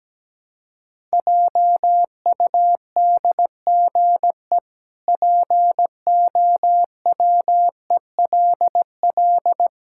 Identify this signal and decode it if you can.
Morse code, heard as a keyed tone, about 17 wpm, 700 Hz. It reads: JUDGE POWELL
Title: Pangram Challenge v2 - 17wpm